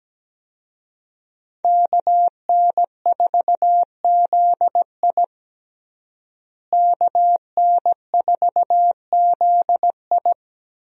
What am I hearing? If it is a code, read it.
Morse code, heard as a keyed tone, about 17 wpm, 700 Hz. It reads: KN4ZI KN4ZI
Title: USA Call Signs - 5 Characters - Repeated 2x - 17wpm